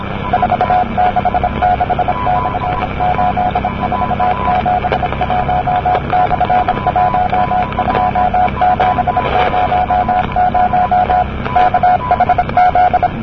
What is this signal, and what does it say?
Morse code, heard as a keyed tone, about 26 wpm, 730 Hz, with radio noise. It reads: 466/8471/12700 KHZ
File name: camerica / ptr / ptr-23-cw-cq-rb-1973